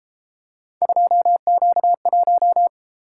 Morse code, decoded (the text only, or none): 2Q1